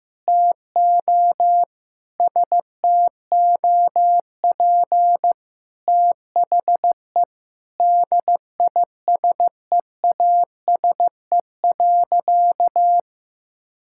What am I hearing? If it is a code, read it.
Morse code, heard as a keyed tone, about 15 wpm, 700 Hz. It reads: TO STOP THE DISEASE.
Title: Sentences from Top 1000 Words - Review Part 6 - 15wpm